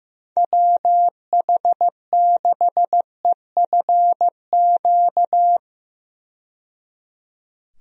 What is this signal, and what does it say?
Morse code, heard as a keyed tone, about 15 wpm, 700 Hz. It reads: WH6EFQ
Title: USA Call Signs - 6 Characters - 15wpm